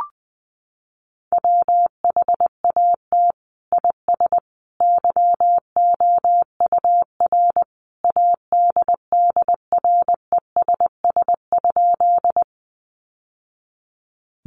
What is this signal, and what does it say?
Morse code, heard as a keyed tone, about 20 wpm, 700 Hz. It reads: WHAT IS YOUR ADDRESS?
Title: Sentences from Top 1000 Words 20wpm